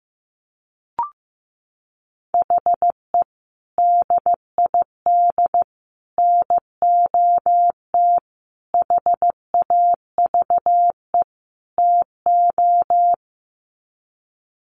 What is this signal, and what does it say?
Morse code, heard as a keyed tone, about 15 wpm, 700 Hz. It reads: HE DID NOT HAVE TO
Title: Sets of 5 Words 15wpm